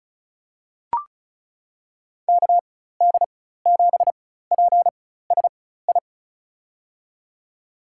Morse code, heard as a keyed tone, about 34 wpm, spaced as for 16 wpm, 700 Hz. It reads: KD7PSI
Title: USA Call Signs - 4x Character Spacing - 35wpm